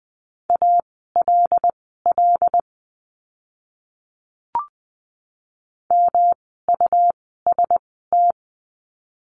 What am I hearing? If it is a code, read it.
Morse code, heard as a keyed tone, about 20 wpm, spaced as for 14 wpm, 700 Hz. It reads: ALL MUST
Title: 237 Lesson 38f - Letter q - Words 20wpm